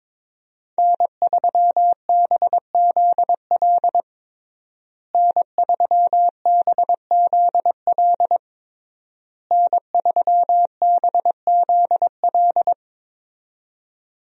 Morse code, heard as a keyed tone, about 22 wpm, 700 Hz. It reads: N3BZL N3BZL N3BZL
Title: USA Call Signs - 5 Characters - Repeated 3x - 22wpm